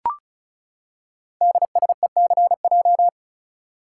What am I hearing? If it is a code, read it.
Morse code, heard as a keyed tone, about 36 wpm, 690 Hz. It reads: DSECJ
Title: Code Groups - 5 Letters - 35wpm